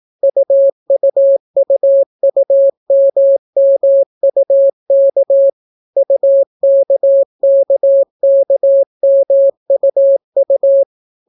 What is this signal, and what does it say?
Morse code, heard as a keyed tone, about 18 wpm, 550 Hz. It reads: UUUUMMUK UKKKMUU